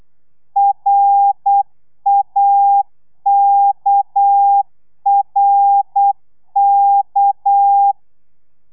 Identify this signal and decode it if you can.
Morse code, heard as a keyed tone, about 8 wpm, 800 Hz. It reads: RAKRK
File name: telegrafia / i2tfj / Lez6 35